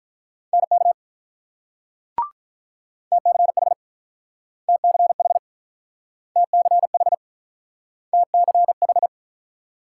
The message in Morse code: NK TCH TCH TCH TCH

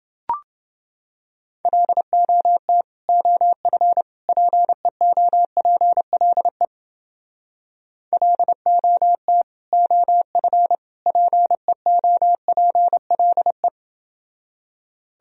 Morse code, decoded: LOT OF PEOPLE LOT OF PEOPLE

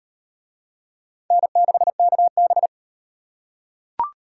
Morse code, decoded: N6KB

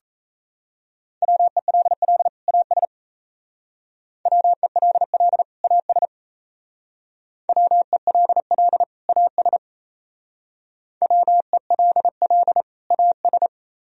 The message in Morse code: WELL AS WELL AS WELL AS WELL AS

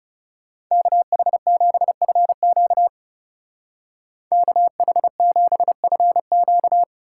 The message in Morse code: KH7FQ KH7FQ

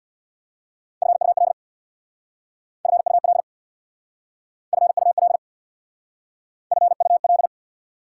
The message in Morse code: FUL FUL FUL FUL

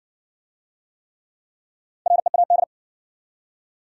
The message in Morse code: READ

